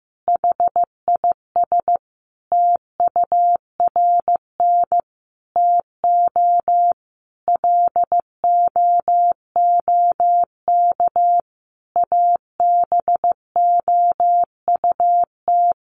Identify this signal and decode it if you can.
Morse code, heard as a keyed tone, about 15 wpm, 700 Hz. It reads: HIS TURN TO LOOK ABOUT
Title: Sentences from Top 200 Words 15wpm